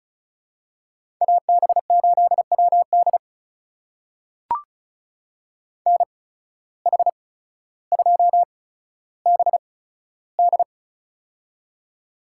Morse code, heard as a keyed tone, about 36 wpm, spaced as for 15 wpm, 700 Hz. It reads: AB8WD NH2BD
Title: USA Call Signs - 5 Characters - 8x Character Spacing - 35wpm